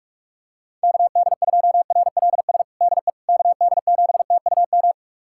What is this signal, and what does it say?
Morse code, heard as a keyed tone, about 42 wpm, 700 Hz. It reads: KD2RLS DE KD7TUM
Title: USA Callsign DE Callsign - 6 Characters - 45wpm